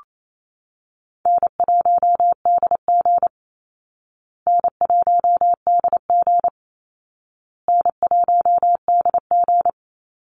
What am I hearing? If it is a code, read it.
Morse code, heard as a keyed tone, about 28 wpm, 700 Hz. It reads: N1DG N1DG N1DG
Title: Easy Call Signs Repeated 3x - 28wpm